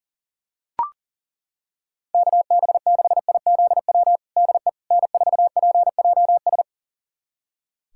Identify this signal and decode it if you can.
Morse code, heard as a keyed tone, about 40 wpm, 700 Hz. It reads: KB6IZW DE N4PJS
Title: USA Callsign DE Callsign 40wpm